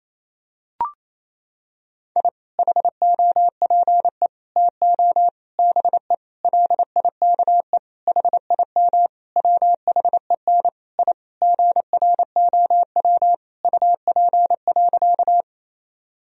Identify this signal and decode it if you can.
Morse code, heard as a keyed tone, about 28 wpm, 700 Hz. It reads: I HOPE TO BE LIKE HIM WHEN I GROW UP.
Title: Sentences from Top 1000 Words - Review Part 1 - 28wpm